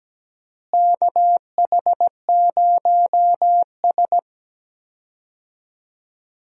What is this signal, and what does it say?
Morse code, heard as a keyed tone, about 17 wpm, 700 Hz. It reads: KH0S
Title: USA Call Signs - 4 Characters - Repeated 3x - 17wpm